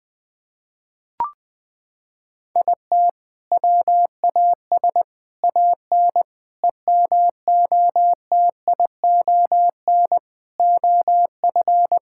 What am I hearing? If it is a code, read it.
Morse code, heard as a keyed tone, about 20 wpm, 700 Hz. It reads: IT WAS AN EMOTION OF
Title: Sentences from Top 1600 Words 20wpm